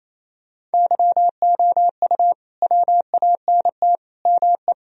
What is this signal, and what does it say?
Morse code, heard as a keyed tone, about 28 wpm, 700 Hz. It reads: YOU WANT ME
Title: Sentences from Top 1000 Words - Review Part 2 - 28wpm